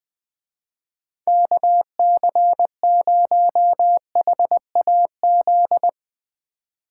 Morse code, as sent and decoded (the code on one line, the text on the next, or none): -.- -.-. ----- .... .- --..
KC0HAZ